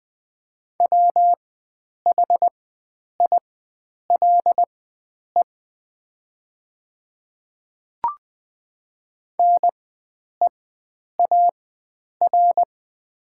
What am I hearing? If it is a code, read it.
Morse code, heard as a keyed tone, about 20 wpm, spaced as for 9 wpm, 700 Hz. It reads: WHILE NEAR
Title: 4x Character Spacing - Top 500 Words - 20wpm